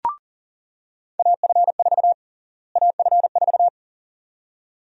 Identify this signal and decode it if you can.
Morse code, heard as a keyed tone, about 40 wpm, 700 Hz. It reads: AF4 AF4
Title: USA Call Sign Prefixes - Repeated 2x - 40wpm